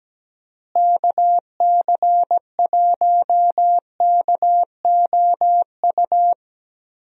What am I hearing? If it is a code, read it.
Morse code, heard as a keyed tone, about 17 wpm, 700 Hz. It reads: KC1KOU
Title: USA Call Signs - 6 Characters - Repeated 3x - 17wpm